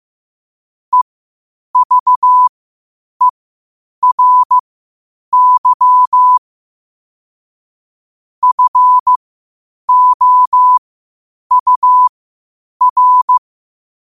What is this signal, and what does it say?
Morse code, heard as a keyed tone, about 15 wpm, spaced as for 8 wpm, 1000 Hz. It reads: EVERY FOUR
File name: qotdfiles / 8wpm